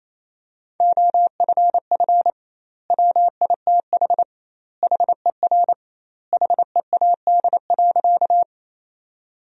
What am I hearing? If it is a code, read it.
Morse code, heard as a keyed tone, about 28 wpm, 700 Hz. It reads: OFF WITH HER HEAD.
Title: 2x Word-Spacing - Sentences from Top 300 Words - 28wpm